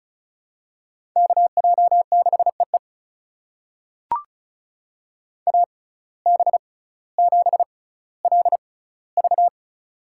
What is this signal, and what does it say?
Morse code, heard as a keyed tone, about 36 wpm, spaced as for 18 wpm, 700 Hz. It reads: KJ6EE AB7LV
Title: USA Call Signs - 5 Characters - 6x Character Spacing - 35wpm